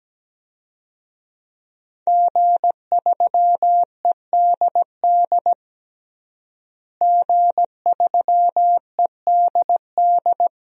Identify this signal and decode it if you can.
Morse code, heard as a keyed tone, about 17 wpm, 700 Hz. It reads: G3EDD G3EDD
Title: DXCC-100 Call Signs - 5 Characters - Repeated 3x - 17wpm